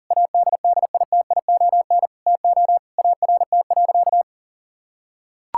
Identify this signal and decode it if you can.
Morse code, heard as a keyed tone, about 40 wpm, 700 Hz. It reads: ADDITION TO ART.
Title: Sentences from Top 1000 Words - Review Part 6 - 40wpm